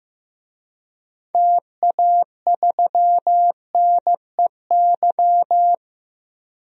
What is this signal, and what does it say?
Morse code, heard as a keyed tone, about 15 wpm, 700 Hz. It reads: TA3NEY
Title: DXCC-100 Call Signs - 6 Characters - 15wpm